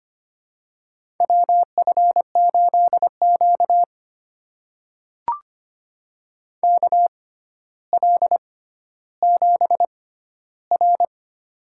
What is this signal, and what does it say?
Morse code, heard as a keyed tone, about 25 wpm, spaced as for 13 wpm, 700 Hz. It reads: WF8Q KL7R
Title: USA Call Signs - 4 Characters - 6x Character Spacing - 25wpm